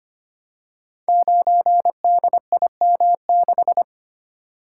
Morse code, heard as a keyed tone, about 25 wpm, 700 Hz. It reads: 9DIM6
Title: Code Groups - 5 Letter:Numbers - 25wpm